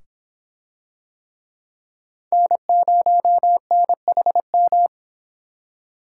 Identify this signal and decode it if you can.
Morse code, heard as a keyed tone, about 22 wpm, 700 Hz. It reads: N0NHM